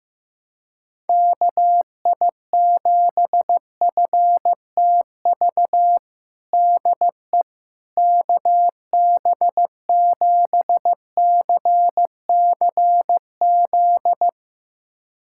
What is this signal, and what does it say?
Morse code, heard as a keyed tone, about 15 wpm, 700 Hz. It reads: KI7FTV DE KB7CCZ
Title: USA Callsign DE Callsign - 6 Characters - 15wpm